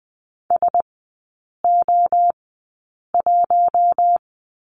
Morse code, heard as a keyed tone, about 20 wpm, spaced as for 8 wpm, 690 Hz. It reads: SO1
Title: 060 Lesson 11e - Letter h - Three Characters 20wpm